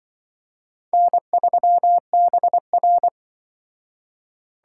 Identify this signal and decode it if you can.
Morse code, heard as a keyed tone, about 24 wpm, 700 Hz. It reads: N3BR
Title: Speed Racing - USA 1x2 Call Signs 20wpm